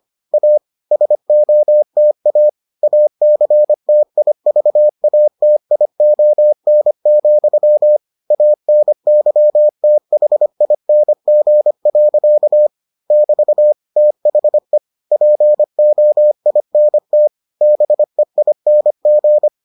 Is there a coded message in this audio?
Yes: Morse code from 0.3 to 19.5 s, heard as a keyed tone, about 25 wpm, 600 Hz. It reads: A SOTA ACTIVATION, ANYTHING. <BT> THE POINT BEING